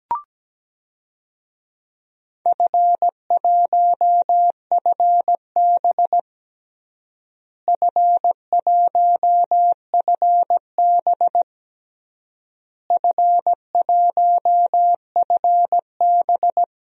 Morse code, heard as a keyed tone, about 17 wpm, 700 Hz. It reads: F1FB F1FB F1FB
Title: DXCC-100 Call Signs - 4 Characters - Repeated 3x - 17wpm